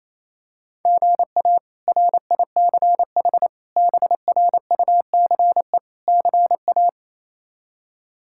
Morse code, heard as a keyed tone, about 27 wpm, 700 Hz. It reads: GA RICH BRUCE CA
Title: K1USN SST QSO Practice - 28wpm